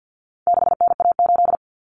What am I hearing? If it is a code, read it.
Morse code, heard as a keyed tone, about 50 wpm, 700 Hz. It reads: 6NA8
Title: Common Cut Numbers - 4-Digit 50wpm